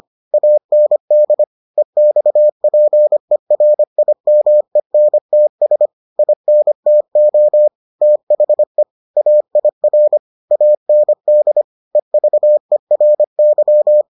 Morse code, heard as a keyed tone, about 25 wpm, 600 Hz. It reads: AND EXPERIMENTS INTO THE AIR AND EVERY